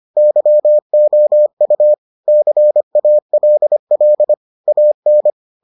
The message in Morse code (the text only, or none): YOU CALL AN